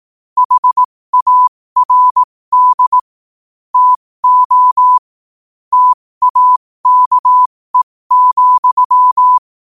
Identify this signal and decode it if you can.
Morse code, heard as a keyed tone, about 18 wpm, spaced as for 15 wpm, 1000 Hz. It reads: HARD TO TAKE,
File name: qotdfiles / 15wpm